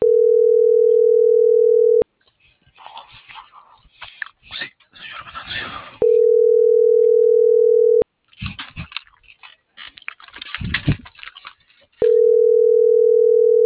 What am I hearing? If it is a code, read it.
Ringback tone (North America), 440 + 480 Hz.